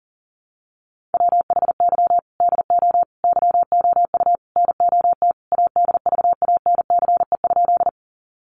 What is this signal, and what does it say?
Morse code, heard as a keyed tone, about 40 wpm, 700 Hz. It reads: WHY DO YOU NOT ADVANCE?
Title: Sentences from Top 1400 Words 40wpm